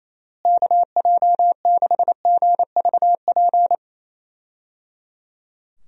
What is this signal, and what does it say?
Morse code, heard as a keyed tone, about 27 wpm, 700 Hz. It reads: KJ6GVP
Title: USA Call Signs - 6 Characters - Repeated 3x - 28wpm